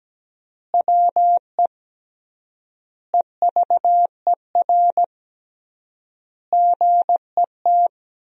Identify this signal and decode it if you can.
Morse code, heard as a keyed tone, about 17 wpm, 700 Hz. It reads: WE EVER GET
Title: 3x Word-Spacing - Sentences from Top 400 Words - 17wpm